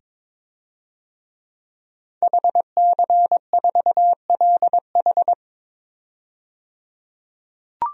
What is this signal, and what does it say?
Morse code, heard as a keyed tone, about 22 wpm, 700 Hz. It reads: HC4LH
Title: DXCC-100 Call Signs - 5 Characters - 22wpm